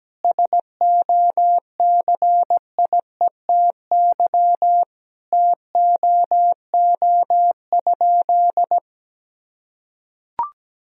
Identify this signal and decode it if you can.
Morse code, heard as a keyed tone, about 17 wpm, 700 Hz. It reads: SOCIETY TOO?
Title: Sentences from Top 1000 Words - Review Part 4 - 17wpm